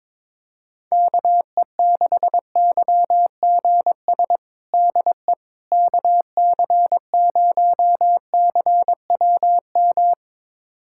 Morse code, heard as a keyed tone, about 21 wpm, 700 Hz. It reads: KE6YGS DE KC0CWM
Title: USA Callsign DE Callsign 22wpm